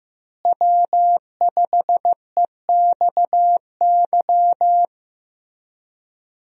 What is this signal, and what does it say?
Morse code, heard as a keyed tone, about 15 wpm, 700 Hz. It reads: W5EXY
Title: USA Call Signs - 5 Characters - Repeated 2x - 15wpm